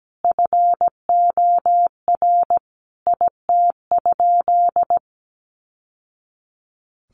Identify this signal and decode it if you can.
Morse code, heard as a keyed tone, about 17 wpm, 700 Hz. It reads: FOR IT?
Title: Sentences from Top 900 Words 17wpm